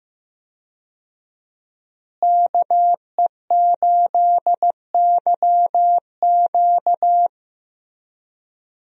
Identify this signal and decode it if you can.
Morse code, heard as a keyed tone, about 16 wpm, 700 Hz. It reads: KE8YQ
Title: USA Call Signs - 5 Characters - Repeated 2x - 15wpm